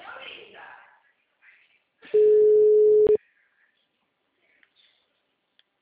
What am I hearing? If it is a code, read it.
Ringback tone (Europe, 425 Hz), with speech.